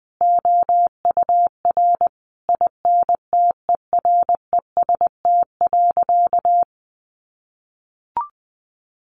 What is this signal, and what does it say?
Morse code, heard as a keyed tone, about 20 wpm, 700 Hz. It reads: OUR INTEREST.
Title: Sentences from Top 1300 Words 20wpm